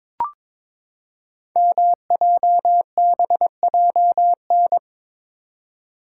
Morse code, heard as a keyed tone, about 23 wpm, 700 Hz. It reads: MJBJN
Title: Code Groups - 5 Letters - 22wpm